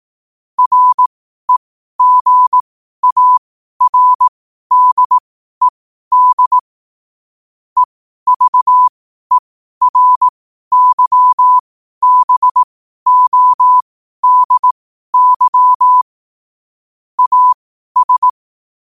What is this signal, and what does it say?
Morse code, heard as a keyed tone, about 18 wpm, spaced as for 12 wpm, 1000 Hz. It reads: REGARDED EVERYBODY AS